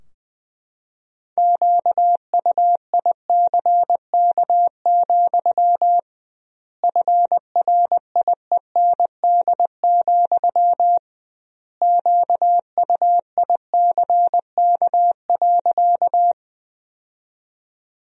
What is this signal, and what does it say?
Morse code, heard as a keyed tone, about 20 wpm, 700 Hz. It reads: QUICK, FRIEND, QUICK.
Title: 2x Word-Spacing - Sentences from Top 500 Words - 20wpm